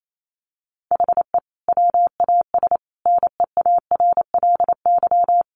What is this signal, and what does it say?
Morse code, heard as a keyed tone, about 28 wpm, 700 Hz. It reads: HE WAS NEARLY
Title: Sentences from Top 1300 Words 28wpm